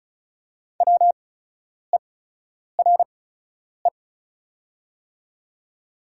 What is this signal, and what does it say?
Morse code, heard as a keyed tone, about 36 wpm, spaced as for 10 wpm, 690 Hz. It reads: WERE
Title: 8x Character Spacing - Top 100 Words - 35wpm